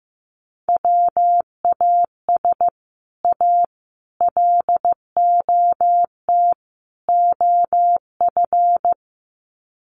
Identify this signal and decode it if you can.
Morse code, heard as a keyed tone, about 14 wpm, 700 Hz. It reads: WAS A LOT OF
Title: Sets of 4 Words 15wpm